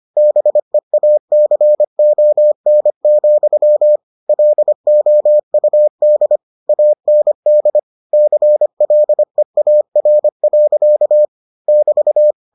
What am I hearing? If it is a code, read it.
Morse code, heard as a keyed tone, about 25 wpm, 600 Hz. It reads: BEACON, LOUD AND CLEAR. <BT>